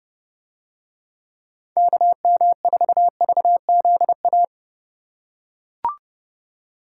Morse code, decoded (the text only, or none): KM4VZA